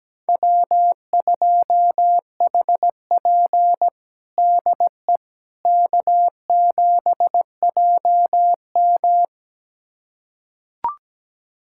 Morse code, heard as a keyed tone, about 17 wpm, 700 Hz. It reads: W2HP DE K7JM
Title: USA Callsign DE Callsign - 4 Characters - 17wpm